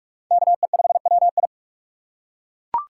Morse code, heard as a keyed tone, about 41 wpm, 700 Hz. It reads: KE5WI
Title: USA Call Signs - 5 Characters - 4x Character Spacing - 45wpm